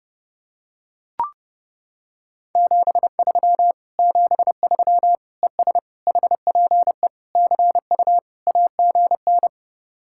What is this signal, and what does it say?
Morse code, heard as a keyed tone, about 28 wpm, 700 Hz. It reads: 73 73 ES HPE CU AGN